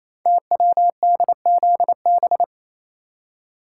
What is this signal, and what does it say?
Morse code, heard as a keyed tone, about 29 wpm, 700 Hz. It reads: TWDZB
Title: Code Groups - 5 Letters - 28wpm